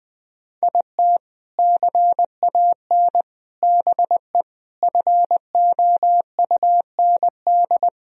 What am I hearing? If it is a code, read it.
Morse code, heard as a keyed tone, about 20 wpm, 700 Hz. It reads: IT CAN BE FOUND
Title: Sentences from Top 1000 Words - Review Part 1 - 20wpm